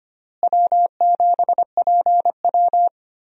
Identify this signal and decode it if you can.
Morse code, heard as a keyed tone, about 25 wpm, 700 Hz. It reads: W7PW
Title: USA Call Signs - 4 Characters - Repeated 3x - 25wpm